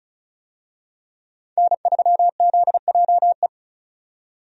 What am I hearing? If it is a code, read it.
Morse code, heard as a keyed tone, about 36 wpm, 700 Hz. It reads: N3ZJE
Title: USA Call Signs - 5 Characters - 35wpm